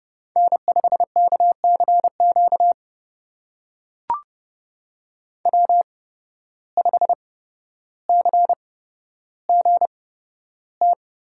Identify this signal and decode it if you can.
Morse code, heard as a keyed tone, about 30 wpm, spaced as for 13 wpm, 700 Hz. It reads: N5KCQ W5CGT
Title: USA Call Signs - 5 Characters - 8x Character Spacing - 30wpm